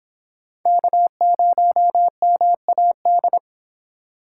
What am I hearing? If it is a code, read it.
Morse code, heard as a keyed tone, about 29 wpm, 700 Hz. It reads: K0MAD